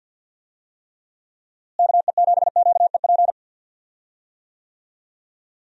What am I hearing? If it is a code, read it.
Morse code, heard as a keyed tone, about 52 wpm, 700 Hz. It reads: KE7QEP
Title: USA Call Signs - 6 Characters - Repeated 3x - 50wpm